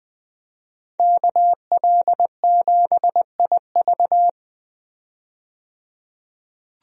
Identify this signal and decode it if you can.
Morse code, heard as a keyed tone, about 20 wpm, 700 Hz. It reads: KL7IV
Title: DXCC-100 Call Signs - 5 Characters - 20wpm